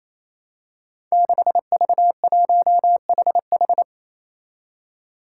Morse code, heard as a keyed tone, about 28 wpm, 700 Hz. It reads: 6V1HH
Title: Five Challenging Letter-Number Combinations 28wpm